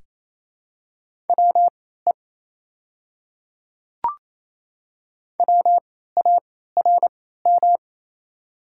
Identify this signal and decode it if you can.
Morse code, heard as a keyed tone, about 28 wpm, spaced as for 16 wpm, 700 Hz. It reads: WE WARM
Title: 3x Character Spacing - Top 500 Words - 28wpm